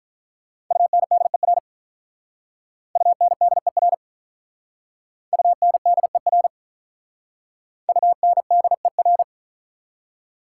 Morse code, heard as a keyed tone, about 45 wpm, 700 Hz. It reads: UNDER UNDER UNDER UNDER